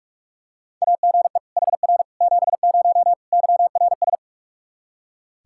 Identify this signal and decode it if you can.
Morse code, heard as a keyed tone, about 41 wpm, 700 Hz. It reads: AGE HR 70 YRS
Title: Practice QSO - Exchange 3 - More Info - 45wpm